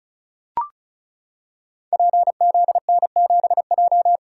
Morse code, heard as a keyed tone, about 35 wpm, 700 Hz. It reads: PZN7J
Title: Code Groups - 5 Letter:Numbers - 35wpm